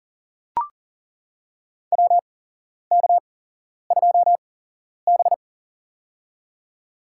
Morse code, heard as a keyed tone, about 40 wpm, spaced as for 11 wpm, 700 Hz. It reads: WK2B
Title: USA Call Signs - 4 Characters - 8x Character Spacing - 40wpm